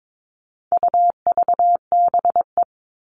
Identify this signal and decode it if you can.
Morse code, heard as a keyed tone, about 22 wpm, 700 Hz. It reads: UVBE
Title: All Cut Numbers - 4-Digit 22wpm